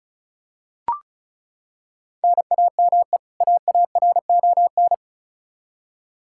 Morse code, heard as a keyed tone, about 36 wpm, 700 Hz. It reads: NAME AARON
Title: Practice QSO - Exchange 1 - Beginning - 35wpm